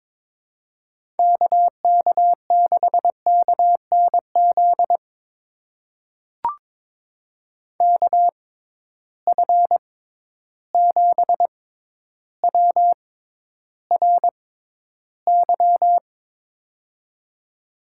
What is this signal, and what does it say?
Morse code, heard as a keyed tone, about 21 wpm, spaced as for 11 wpm, 700 Hz. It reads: KK6KNZ KF7WRY
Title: USA Call Signs - 6 Characters - 6x Character Spacing - 22wpm